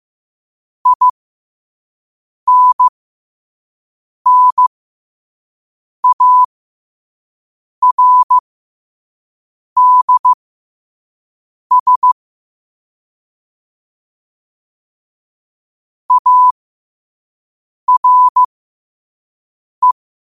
Morse code, heard as a keyed tone, about 15 wpm, spaced as for 5 wpm, 1000 Hz. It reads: INNARDS ARE
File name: qotdfiles / 5wpm